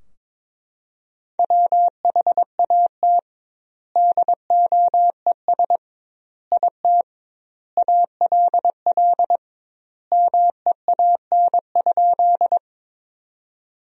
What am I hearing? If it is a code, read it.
Morse code, heard as a keyed tone, about 21 wpm, 700 Hz. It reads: WHAT DOES IT ALL MEAN?
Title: 2x Word-Spacing - Sentences from Top 200 Words - 22wpm